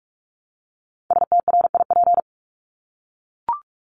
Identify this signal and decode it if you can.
Morse code, heard as a keyed tone, about 44 wpm, 700 Hz. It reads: STRIP
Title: Top 1501-1750 Words 45wpm